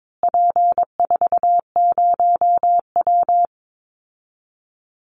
Morse code, heard as a keyed tone, about 22 wpm, 700 Hz. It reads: P40W
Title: Call Signs Repeated 2x - 22wpm